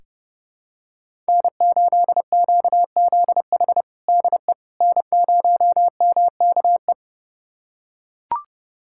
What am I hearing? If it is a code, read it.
Morse code, heard as a keyed tone, about 30 wpm, 700 Hz. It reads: N8QZH DE N0MKE
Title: USA Callsign DE Callsign - 5 Characters - 30wpm